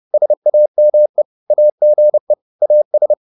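Morse code, heard as a keyed tone, about 30 wpm, 600 Hz. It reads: SAME AGE AS
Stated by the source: Morse code at 29 wpm, sending Ken Follett's novel Never in English, from CW med Gnister.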